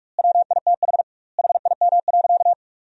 Morse code, heard as a keyed tone, about 44 wpm, 700 Hz. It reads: WITH HIM.
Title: 2x Word-Spacing - Sentences from Top 300 Words - 45wpm